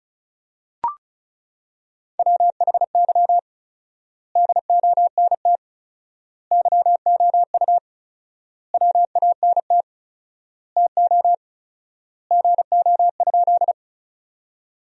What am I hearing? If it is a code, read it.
Morse code, heard as a keyed tone, about 34 wpm, 700 Hz. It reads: WHY DONT YOU WANT TO GO?
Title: 4x Word-Spacing - Sentences from Top 300 Words - 35wpm